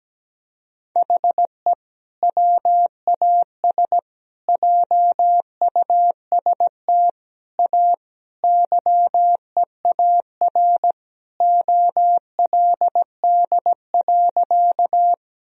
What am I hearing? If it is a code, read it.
Morse code, heard as a keyed tone, about 17 wpm, 700 Hz. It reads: HE WAS JUST A YEAR OLD.
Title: Sentences from Top 1000 Words - Review Part 1 - 17wpm